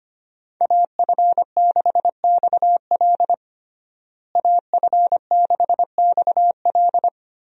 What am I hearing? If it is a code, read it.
Morse code, heard as a keyed tone, about 25 wpm, 700 Hz. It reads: AF6XL AF6XL
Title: USA Call Signs - 5 Characters - Repeated 3x - 25wpm